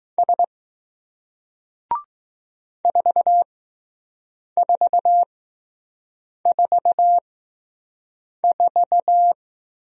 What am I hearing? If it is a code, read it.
Morse code, heard as a keyed tone, about 20 wpm, spaced as for 6 wpm, 700 Hz. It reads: S 4444